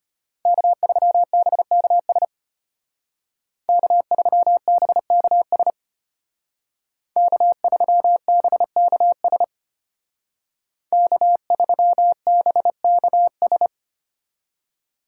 Morse code, heard as a keyed tone, about 30 wpm, 700 Hz. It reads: K3BKS K3BKS K3BKS K3BKS
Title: Speed Racing - USA 1x3 Call Signs 25wpm